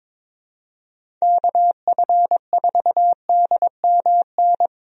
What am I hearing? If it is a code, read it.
Morse code, heard as a keyed tone, about 22 wpm, 700 Hz. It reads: KF4DMN
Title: USA Call Signs - 6 Characters - Repeated 3x - 22wpm